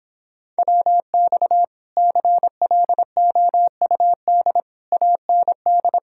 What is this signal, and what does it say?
Morse code, heard as a keyed tone, about 26 wpm, 700 Hz. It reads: WX CLOUD AND